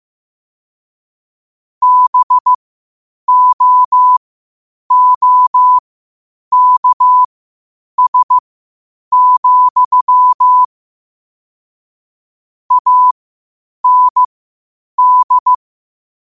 Morse code, heard as a keyed tone, about 15 wpm, spaced as for 8 wpm, 1000 Hz. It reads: BOOKS, AND
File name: qotdfiles / 8wpm